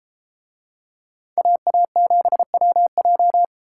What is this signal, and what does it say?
Morse code, heard as a keyed tone, about 32 wpm, 700 Hz. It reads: AA7WJ